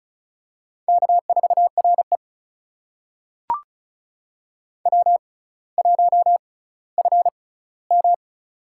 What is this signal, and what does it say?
Morse code, heard as a keyed tone, about 36 wpm, spaced as for 18 wpm, 700 Hz. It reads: K4RE W1FM
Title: USA Call Signs - 4 Characters - 6x Character Spacing - 35wpm